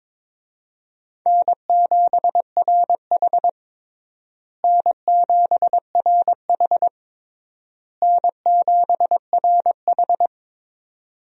Morse code, heard as a keyed tone, about 22 wpm, 700 Hz. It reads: N7RH N7RH N7RH
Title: USA Call Signs - 4 Characters - Repeated 3x - 22wpm